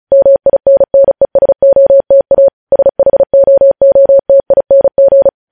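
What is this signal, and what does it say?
Morse code, heard as a keyed tone, about 36 wpm, 550 Hz. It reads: MINNESOTA SHOOTING
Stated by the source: Morse news headlines practice at 35.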